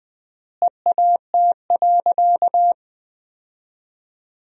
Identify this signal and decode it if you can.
Morse code, heard as a keyed tone, about 21 wpm, 700 Hz. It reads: EAT.
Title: 3x Word-Spacing - Sentences from Top 300 Words - 20wpm